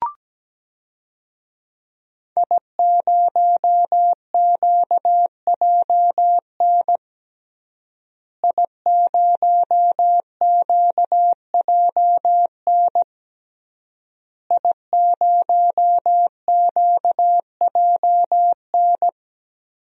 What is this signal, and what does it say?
Morse code, heard as a keyed tone, about 17 wpm, 700 Hz. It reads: I0QJN I0QJN I0QJN
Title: DXCC-100 Call Signs - 5 Characters - Repeated 3x - 17wpm